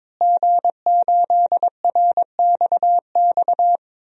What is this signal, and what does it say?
Morse code, heard as a keyed tone, about 22 wpm, 700 Hz. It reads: G8RXX
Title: DXCC-100 Call Signs - 5 Characters - Repeated 2x - 22wpm